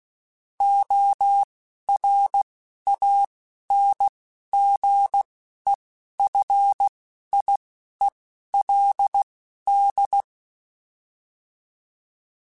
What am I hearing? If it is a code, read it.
Morse code, heard as a keyed tone, about 16 wpm, spaced as for 11 wpm, 800 Hz. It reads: ORANGEFIELD